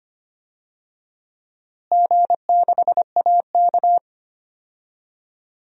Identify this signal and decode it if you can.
Morse code, heard as a keyed tone, about 25 wpm, 700 Hz. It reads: G6AK
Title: DXCC-100 Call Signs - 4 Characters - Repeated 2x - 25wpm